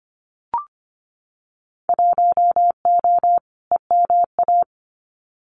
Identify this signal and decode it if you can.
Morse code, heard as a keyed tone, about 25 wpm, 700 Hz. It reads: 1O EMA
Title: Winter Field Day Exchanges 25wpm